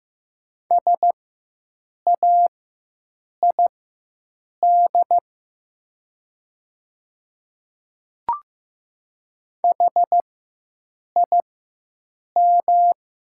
Morse code, heard as a keyed tone, about 15 wpm, spaced as for 7 wpm, 700 Hz. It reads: SAID HIM
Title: 4x Character Spacing - Top 100 Words - 15wpm